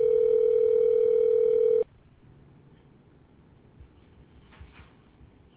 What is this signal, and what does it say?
Ringback tone (North America), 440 + 480 Hz.